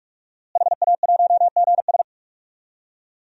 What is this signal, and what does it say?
Morse code, heard as a keyed tone, about 41 wpm, 700 Hz. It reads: HA1GS
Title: DXCC-100 Call Signs - 5 Characters - Repeated 2x - 45wpm